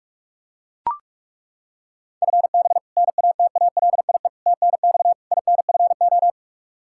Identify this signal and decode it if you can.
Morse code, heard as a keyed tone, about 41 wpm, 700 Hz. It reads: FB NATALIE TNX INFO
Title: Practice QSO - Exchange 3 - More Info - 45wpm